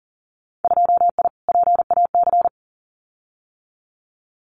Morse code, heard as a keyed tone, about 42 wpm, 700 Hz. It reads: 2I PAC
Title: Winter Field Day Exchanges Encore 40wpm